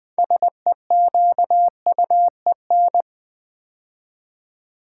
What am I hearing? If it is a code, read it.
Morse code, heard as a keyed tone, about 20 wpm, 700 Hz. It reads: SEQUEN
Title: Build A Word - 3 Letters At A Time - 8-Letter Words - 20wpm